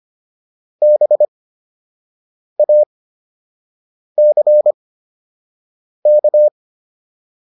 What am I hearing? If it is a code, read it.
Morse code, heard as a keyed tone, about 25 wpm, spaced as for 6 wpm, 600 Hz. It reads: BACK